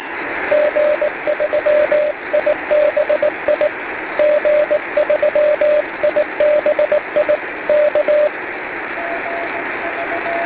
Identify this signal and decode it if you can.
Morse code, heard as a keyed tone, about 18 wpm, 600 Hz, with radio noise. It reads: G3IBI G3IBI K